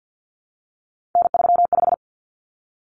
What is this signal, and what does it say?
Morse code, heard as a keyed tone, about 50 wpm, 700 Hz. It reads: N35
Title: Common Cut Numbers - 3-Digit 50wpm